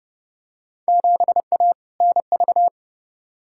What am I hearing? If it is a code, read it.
Morse code, heard as a keyed tone, about 30 wpm, 700 Hz. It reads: 7A NV
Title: Field Day Exchanges Encore - 30wpm